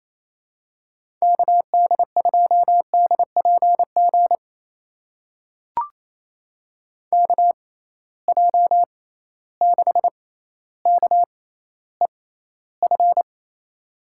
Morse code, heard as a keyed tone, about 27 wpm, spaced as for 14 wpm, 700 Hz. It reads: KD2DPG KJ6KEF
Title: USA Call Signs - 6 Characters - 6x Character Spacing - 28wpm